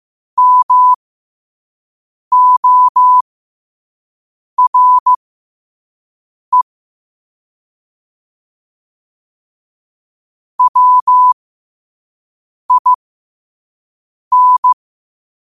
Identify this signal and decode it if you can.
Morse code, heard as a keyed tone, about 15 wpm, spaced as for 5 wpm, 1000 Hz. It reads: MORE WIN
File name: qotdfiles / 5wpm